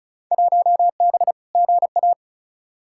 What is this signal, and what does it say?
Morse code, heard as a keyed tone, about 34 wpm, 700 Hz. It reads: 1B GA
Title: Field Day Exchanges Encore - 35wpm